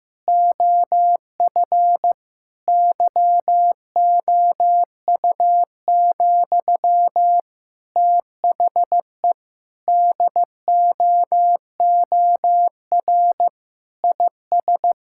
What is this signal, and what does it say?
Morse code, heard as a keyed tone, about 15 wpm, 700 Hz. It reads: OF YOU, THE DOOR IS
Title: Sentences from Top 1000 Words - Review Part 6 - 15wpm